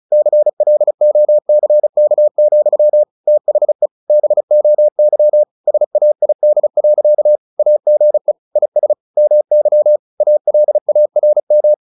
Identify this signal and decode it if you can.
Morse code, heard as a keyed tone, about 34 wpm, 600 Hz. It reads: CLOCK, THE BOY SAID. AGE IS MY ALARM
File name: the old man and the sea / the old man and the sea 1 35wpm